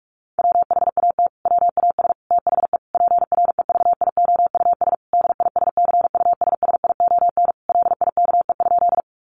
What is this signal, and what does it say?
Morse code, heard as a keyed tone, about 44 wpm, 700 Hz. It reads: WHAT WAS THE PREVIOUS DISCUSSION LIKE?
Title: Sentences from Top 1500 Words 45wpm